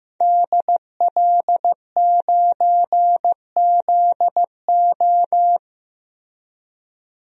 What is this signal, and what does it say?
Morse code, heard as a keyed tone, about 15 wpm, 700 Hz. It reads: DL9ZO
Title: DXCC-100 Call Signs - 5 Characters - Repeated 2x - 15wpm